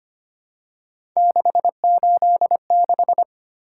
Morse code, heard as a keyed tone, about 25 wpm, 700 Hz. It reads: 686
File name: Three Letter-Number Combinations - 25wpm